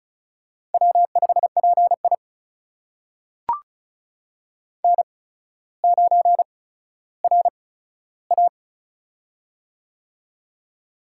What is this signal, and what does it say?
Morse code, heard as a keyed tone, about 36 wpm, spaced as for 15 wpm, 700 Hz. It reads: W5PI N9RA
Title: USA Call Signs - 4 Characters - 8x Character Spacing - 35wpm